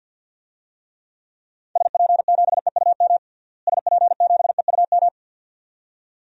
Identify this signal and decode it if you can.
Morse code, heard as a keyed tone, about 49 wpm, 700 Hz. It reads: SP7EUM SP7EUM
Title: DXCC-100 Call Signs - 6 Characters - Repeated 2x - 50wpm